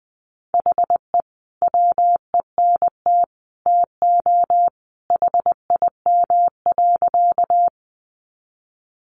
Morse code, heard as a keyed tone, about 20 wpm, 700 Hz. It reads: HE WENT TO HIM.
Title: Sentences from Top 200 Words 20wpm